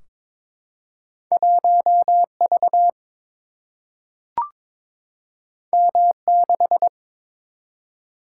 Morse code, heard as a keyed tone, about 23 wpm, 700 Hz. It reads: 1 V M 6
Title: Single-Letter Single-Number 22wpm